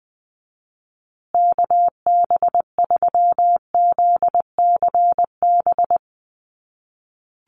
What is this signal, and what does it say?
Morse code, heard as a keyed tone, about 20 wpm, 700 Hz. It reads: KB3ZCB
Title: USA Call Signs Repeated 2x 20wpm